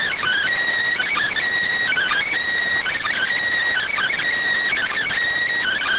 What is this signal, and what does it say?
Emergency Alert System (SAME) tones (header bursts), heard as harsh buzzing bursts. It is heard from 0.4 to 5.2 s.